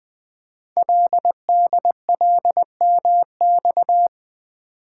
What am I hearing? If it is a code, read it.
Morse code, heard as a keyed tone, about 20 wpm, 700 Hz. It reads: LDLMX
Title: Code Groups - 5 Letters - 20wpm